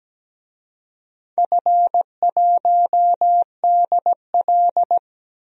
Morse code, heard as a keyed tone, about 17 wpm, 700 Hz. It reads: F1DL
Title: DXCC-100 Call Signs - 4 Characters - 17wpm